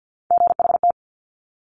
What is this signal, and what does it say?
Morse code, heard as a keyed tone, about 50 wpm, 700 Hz. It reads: GHT
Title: Top Three Letter Combinations in Words 50wpm